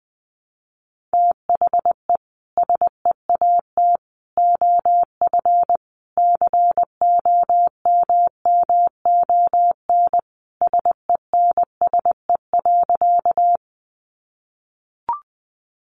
Morse code, heard as a keyed tone, about 20 wpm, 700 Hz. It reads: THE SEAT OF COMMON SENSE.
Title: Sentences from Top 1500 Words 20wpm